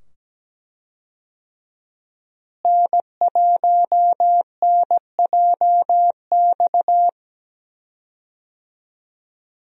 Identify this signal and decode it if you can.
Morse code, heard as a keyed tone, about 17 wpm, 700 Hz. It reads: N1NJX
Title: USA Call Signs - 5 Characters - Repeated 2x - 17wpm